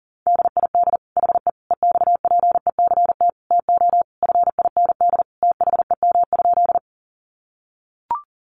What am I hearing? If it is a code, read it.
Morse code, heard as a keyed tone, about 40 wpm, 700 Hz. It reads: DID HE EXPECT TO FIND THEM?
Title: Sentences from Top 1100 Words 40wpm